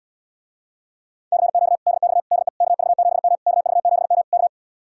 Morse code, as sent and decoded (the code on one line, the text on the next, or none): -.-. --.- / -.. -..- / -.. / -.- ...- --... -- / -.- ...- --... -- / -.-
CQ DX DE KV7M KV7M K